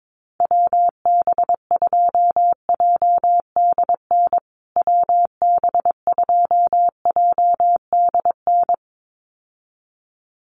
Morse code, heard as a keyed tone, about 22 wpm, 700 Hz. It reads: WB2JDN WB2JDN
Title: USA Call Signs Repeated 2x 22wpm